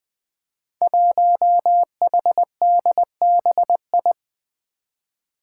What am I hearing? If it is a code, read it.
Morse code, heard as a keyed tone, about 20 wpm, 700 Hz. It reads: 1HDBI
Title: Five Challenging Letter-Number Combinations 20wpm